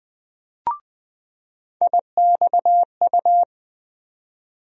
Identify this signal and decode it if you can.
Morse code, heard as a keyed tone, about 20 wpm, 700 Hz. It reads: IXU
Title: USA Call Sign Postfixes - 3 Letters 20wpm